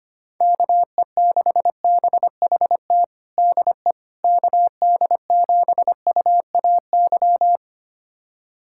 Morse code, heard as a keyed tone, about 25 wpm, 700 Hz. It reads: KE6BHT DE KD7UAY
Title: USA Callsign DE Callsign 25wpm